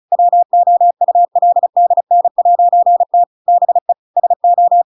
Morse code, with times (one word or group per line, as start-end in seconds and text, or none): 0.1-3.2 WOULDN'T
3.5-3.9 BE
4.2-4.8 SO